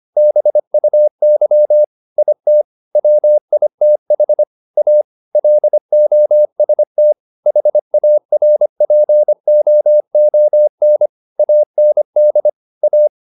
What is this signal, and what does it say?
Morse code, heard as a keyed tone, about 25 wpm, 600 Hz. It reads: BUY IT WITH A LOST HARPOON AND A